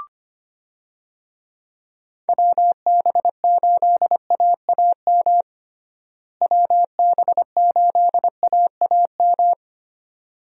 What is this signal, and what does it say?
Morse code, heard as a keyed tone, about 25 wpm, 700 Hz. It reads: WB8AAM WB8AAM
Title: USA Call Signs - 6 Characters - Repeated 2x - 25wpm